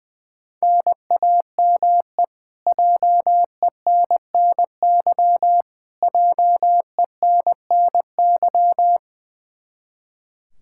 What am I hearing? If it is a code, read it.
Morse code, heard as a keyed tone, about 20 wpm, 700 Hz. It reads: NAME JENNY JENNY
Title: Practice QSO - Exchange 1 - Beginning - 20wpm